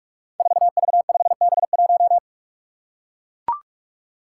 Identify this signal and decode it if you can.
Morse code, heard as a keyed tone, about 41 wpm, 700 Hz. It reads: 4V5B1
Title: Five Challenging Letter-Number Combinations 45wpm